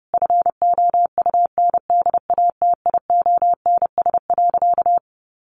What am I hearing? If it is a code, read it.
Morse code, heard as a keyed tone, about 30 wpm, 700 Hz. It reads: FOUNDATIONS.
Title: A Sound Of Thunder 30wpm